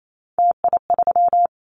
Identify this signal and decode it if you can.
Morse code, heard as a keyed tone, about 28 wpm, 700 Hz. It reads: TI3
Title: Callsign Prefixes for Top 100 DXCC - 28wpm